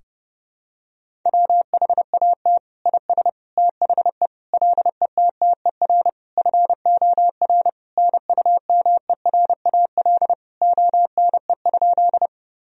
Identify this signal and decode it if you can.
Morse code, heard as a keyed tone, about 30 wpm, 700 Hz. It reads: WHAT IS THE LETTER FOR NUMERAL ONE?
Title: Sentences from Top 1000 Words - Review Part 2 - 30wpm